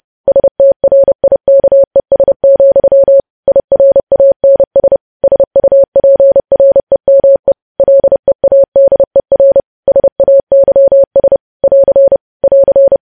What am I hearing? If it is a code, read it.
Morse code, heard as a keyed tone, about 30 wpm, 550 Hz. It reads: STRIKES, IRANS SUPREME LEADER SAYS <AR> <AR>
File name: podcast / News headlines-1x-30x30